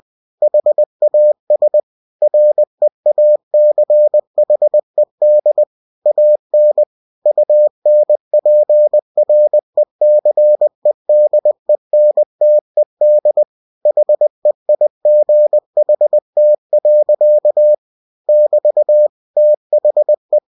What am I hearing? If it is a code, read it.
Morse code, heard as a keyed tone, about 20 wpm, 600 Hz. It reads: HAS REACHED AN UNPRECEDENTED HEIGHT. <BT> THE
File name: cw-books / winnie-the-pooh / winnie-20wpm-page-0096